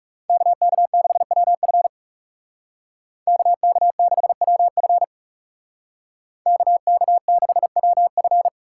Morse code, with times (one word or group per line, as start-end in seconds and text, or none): 0.3-1.9 KK6WF
3.3-5.0 KK6WF
6.5-8.5 KK6WF